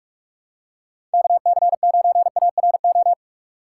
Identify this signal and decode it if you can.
Morse code, heard as a keyed tone, about 46 wpm, 700 Hz. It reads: KC9ARO
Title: USA Call Signs - 3x Character Spacing - 45wpm